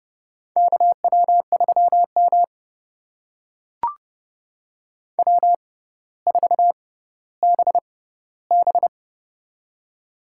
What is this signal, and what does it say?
Morse code, heard as a keyed tone, about 29 wpm, spaced as for 15 wpm, 700 Hz. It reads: KW3M W4BB
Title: USA Call Signs - 4 Characters - 6x Character Spacing - 30wpm